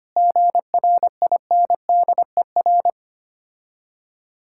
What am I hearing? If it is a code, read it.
Morse code, heard as a keyed tone, about 25 wpm, 690 Hz. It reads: GRINDER
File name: practice / 25wpm / Difficult Words with Hints - 25wpm